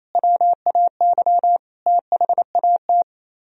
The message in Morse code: WAY THAT

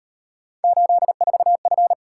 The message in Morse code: 84F

